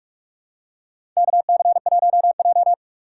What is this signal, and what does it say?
Morse code, heard as a keyed tone, about 46 wpm, 700 Hz. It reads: KC1J
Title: USA Call Signs - 4 Characters - 4x Character Spacing - 45wpm